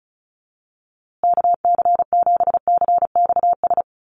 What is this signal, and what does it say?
Morse code, heard as a keyed tone, about 34 wpm, 700 Hz. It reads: KC7CXS
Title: USA Call Signs Repeated 3x 35wpm